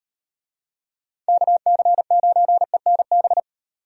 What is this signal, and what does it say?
Morse code, heard as a keyed tone, about 38 wpm, 700 Hz. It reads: KC9ENB